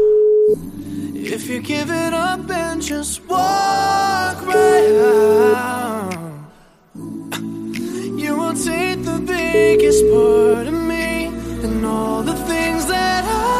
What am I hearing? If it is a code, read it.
Ringback tone (Europe, 425 Hz), with noise.